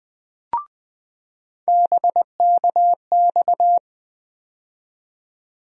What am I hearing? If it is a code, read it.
Morse code, heard as a keyed tone, about 20 wpm, 700 Hz. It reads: BKX
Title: USA Call Sign Postfixes - 3 Letters 20wpm